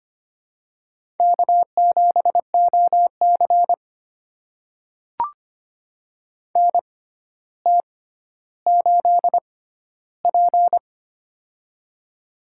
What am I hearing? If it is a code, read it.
Morse code, heard as a keyed tone, about 26 wpm, spaced as for 13 wpm, 700 Hz. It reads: K7OC NT8P
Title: USA Call Signs - 4 Characters - 6x Character Spacing - 25wpm